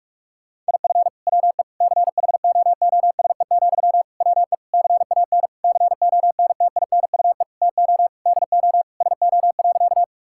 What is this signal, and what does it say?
Morse code, heard as a keyed tone, about 46 wpm, 700 Hz. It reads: IF WE CHOOSE, WE CAN CONTINUE TO DO SO.
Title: Sentences from Top 1000 Words - Review Part 6 - 45wpm